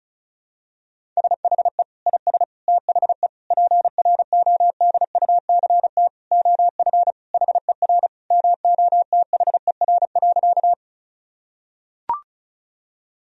Morse code, heard as a keyed tone, about 36 wpm, 700 Hz. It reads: SHE IS THE PRODUCT OF HER MOTHER.
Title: Sentences from Top 1000 Words - Review Part 2 - 35wpm